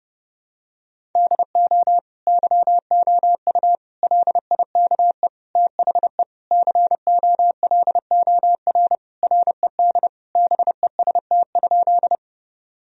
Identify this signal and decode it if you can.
Morse code, heard as a keyed tone, about 32 wpm, 700 Hz. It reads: DO YOU LIKE THE COLOR RED BEST?
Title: Sentences from Top 1000 Words - Review Part 2 - 30wpm